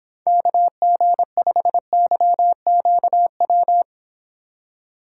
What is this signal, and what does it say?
Morse code, heard as a keyed tone, about 26 wpm, 700 Hz. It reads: KG5YQW